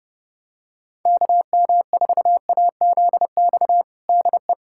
Morse code, heard as a keyed tone, about 30 wpm, 700 Hz. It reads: KM4AZX DE
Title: USA Callsign DE Callsign - 6 Characters - 30wpm